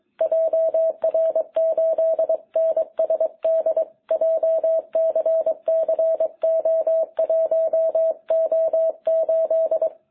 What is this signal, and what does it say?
Morse code, heard as a keyed tone, about 22 wpm, 660 Hz. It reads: JR8NSD JCCO1O8
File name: jh8cbh / 2021cw